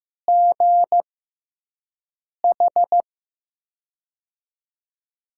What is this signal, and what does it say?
Morse code, heard as a keyed tone, about 15 wpm, spaced as for 5 wpm, 700 Hz. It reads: GH
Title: USA Call Signs - 6 Characters - 6x Character Spacing - 15wpm